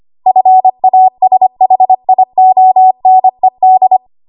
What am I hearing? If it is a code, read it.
Morse code, heard as a keyed tone, about 25 wpm, 750 Hz. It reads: FASHIONED
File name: Morse / 25WPM